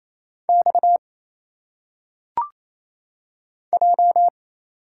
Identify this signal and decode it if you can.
Morse code, heard as a keyed tone, about 27 wpm, 700 Hz. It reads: X J
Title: Single Letters - Phonetic Pronunciation 28wpm